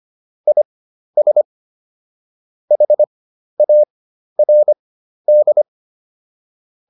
Morse code, heard as a keyed tone, about 25 wpm, spaced as for 12 wpm, 600 Hz. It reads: IS HARD